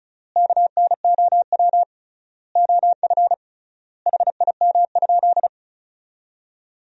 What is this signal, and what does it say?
Morse code, heard as a keyed tone, about 34 wpm, 700 Hz. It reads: KNOW OF HIM?
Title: 3x Word-Spacing - Sentences from Top 100 Words - 35wpm